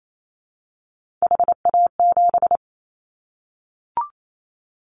Morse code, heard as a keyed tone, about 28 wpm, 700 Hz. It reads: HA7
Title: Callsign Prefixes for Top 25 DXCC - 28wpm